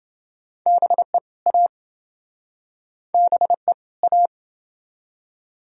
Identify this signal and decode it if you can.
Morse code, heard as a keyed tone, about 29 wpm, 700 Hz. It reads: BE A BE A